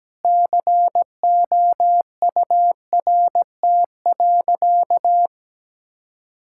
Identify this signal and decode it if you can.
Morse code, heard as a keyed tone, about 17 wpm, 700 Hz. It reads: COURT.
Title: Sentences from Top 1600 Words 17wpm